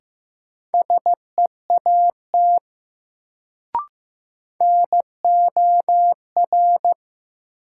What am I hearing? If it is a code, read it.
Morse code, heard as a keyed tone, about 15 wpm, 700 Hz. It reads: SEAT NOR
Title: Sentences from Top 1500 Words - New Word Drill Alt - 15wpm